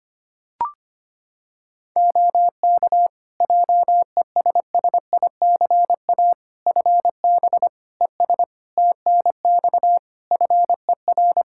Morse code, heard as a keyed tone, about 25 wpm, 700 Hz. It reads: OK JESSICA FB ES TNX FER
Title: Practice QSO - Exchange 2 - Info - 25wpm